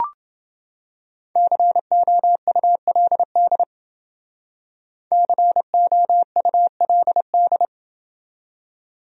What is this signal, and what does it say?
Morse code, heard as a keyed tone, about 30 wpm, 700 Hz. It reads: COULD COULD